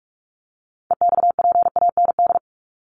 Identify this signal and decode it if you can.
Morse code, heard as a keyed tone, about 44 wpm, 700 Hz. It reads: EXPAND
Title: Top 1251-1500 Words 45wpm